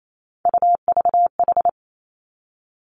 Morse code, heard as a keyed tone, about 28 wpm, 700 Hz. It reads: UVH
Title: Three Challenging Letter-Number Combinations 28wpm